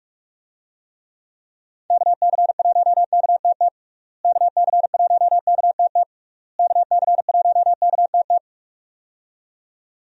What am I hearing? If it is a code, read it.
Morse code, heard as a keyed tone, about 41 wpm, 700 Hz. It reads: KC1KTT KC1KTT KC1KTT
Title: USA Call Signs - 6 Characters - Repeated 3x - 45wpm